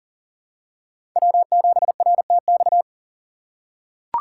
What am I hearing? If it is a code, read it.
Morse code, heard as a keyed tone, about 40 wpm, 700 Hz. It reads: W7RTX
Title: USA Call Signs - 6x Character Spacing - 40wpm